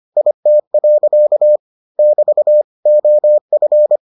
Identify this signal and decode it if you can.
Morse code, heard as a keyed tone, about 25 wpm, 600 Hz. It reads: IT. <BT> OF